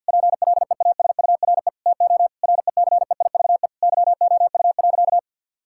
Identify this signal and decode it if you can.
Morse code, heard as a keyed tone, about 50 wpm, 700 Hz. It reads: PLEASURE TO RECEIVE YOU.
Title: Sentences from Top 1600 Words 50wpm